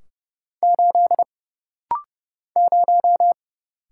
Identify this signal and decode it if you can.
Morse code, heard as a keyed tone, about 30 wpm, 700 Hz. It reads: 8 0
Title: Single Letter-Number - Rapid-Fire 30wpm